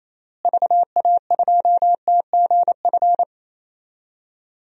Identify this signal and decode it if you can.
Morse code, heard as a keyed tone, about 28 wpm, 700 Hz. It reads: VA2TGF
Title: DXCC-100 Call Signs - 6 Characters - Repeated 3x - 28wpm